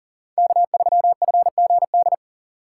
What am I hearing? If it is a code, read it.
Morse code, heard as a keyed tone, about 40 wpm, 700 Hz. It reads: K3FGD
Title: USA Call Signs - 3x Character Spacing - 40wpm